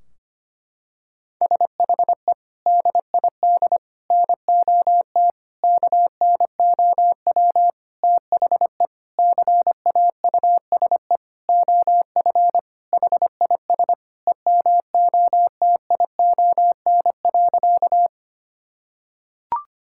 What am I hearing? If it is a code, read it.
Morse code, heard as a keyed tone, about 25 wpm, 700 Hz. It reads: SHE DID NOT KNOW THE CAUSE OF HIS EMOTION.
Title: Sentences from Top 1600 Words 25wpm